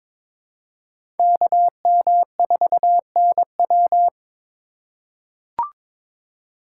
Morse code, heard as a keyed tone, about 21 wpm, 700 Hz. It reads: KM4NW
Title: USA Call Signs - 4x Character Spacing - 22wpm